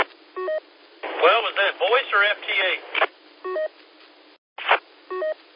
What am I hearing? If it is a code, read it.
Speech. There is no code.